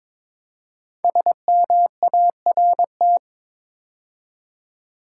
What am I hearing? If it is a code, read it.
Morse code, heard as a keyed tone, about 22 wpm, 700 Hz. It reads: SMART
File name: 5 Letter Words - First 3 Letters Given - 22wpm